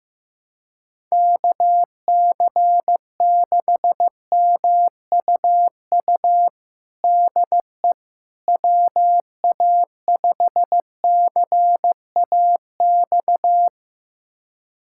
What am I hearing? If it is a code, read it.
Morse code, heard as a keyed tone, about 16 wpm, 700 Hz. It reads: KC6MUU DE WA5CAX
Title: USA Callsign DE Callsign - 6 Characters - 15wpm